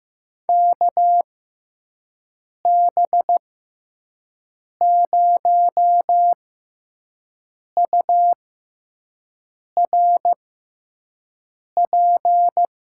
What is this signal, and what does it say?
Morse code, heard as a keyed tone, about 15 wpm, spaced as for 5 wpm, 700 Hz. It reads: KB0URP
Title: USA Call Signs - 6 Characters - 6x Character Spacing - 15wpm